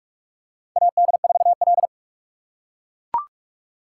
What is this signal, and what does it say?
Morse code, heard as a keyed tone, about 44 wpm, 700 Hz. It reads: AD4L
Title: USA Call Signs - 4 Characters - 2x Character Spacing - 45wpm